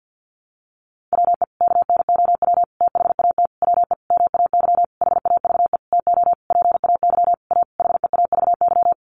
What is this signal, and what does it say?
Morse code, heard as a keyed tone, about 50 wpm, 700 Hz. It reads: WE KNOW THAT WE MAY HAVE TO PAY A HEAVY
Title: Sentences from Top 1100 Words 50wpm